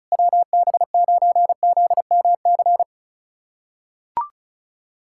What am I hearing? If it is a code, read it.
Morse code, heard as a keyed tone, about 36 wpm, 700 Hz. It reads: WB9ZMC
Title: USA Callsign DE Callsign 35wpm